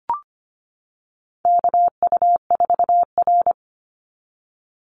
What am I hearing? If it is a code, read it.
Morse code, heard as a keyed tone, about 25 wpm, 700 Hz. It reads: KU4R
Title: NA SOTA Chaser Callsigns 25wpm